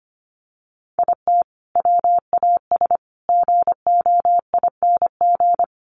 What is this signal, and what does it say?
Morse code, heard as a keyed tone, about 25 wpm, 700 Hz. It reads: IT WAS GOING
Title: Sets of 4 Words 25wpm